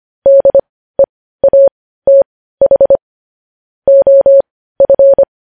Morse code, heard as a keyed tone, about 25 wpm, spaced as for 15 wpm, 550 Hz. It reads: DEATH OF
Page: news headlines in morse code at 15 wpm